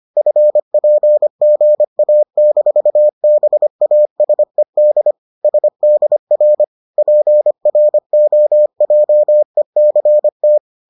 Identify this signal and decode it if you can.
Morse code, heard as a keyed tone, about 25 wpm, 600 Hz. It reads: FPGA-BASED SDR PROJECT